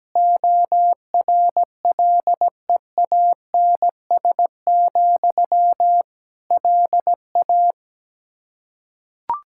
Morse code, heard as a keyed tone, about 17 wpm, 700 Hz. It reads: ORLEANS, LA
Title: K1USN SST QSO Practice 17wpm